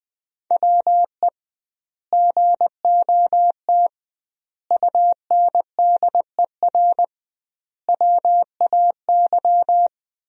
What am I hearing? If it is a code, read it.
Morse code, heard as a keyed tone, about 20 wpm, 700 Hz. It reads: WE GOT UNDER WAY
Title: 2x Word-Spacing - Sentences from Top 300 Words - 20wpm